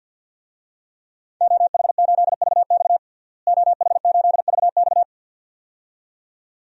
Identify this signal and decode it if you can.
Morse code, heard as a keyed tone, about 52 wpm, 700 Hz. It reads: OH8VX OH8VX
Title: DXCC-100 Call Signs - 5 Characters - Repeated 2x - 50wpm